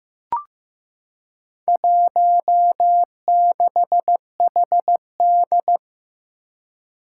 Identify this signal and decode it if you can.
Morse code, heard as a keyed tone, about 15 wpm, 700 Hz. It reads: 16HD
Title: Four Challenging Letter-Number Combinations 15wpm